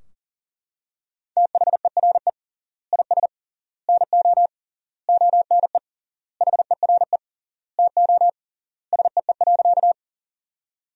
Morse code, heard as a keyed tone, about 40 wpm, 700 Hz. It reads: THERE IS NO ONE HERE TO SEE.
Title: 3x Word-Spacing - Sentences from Top 200 Words - 40wpm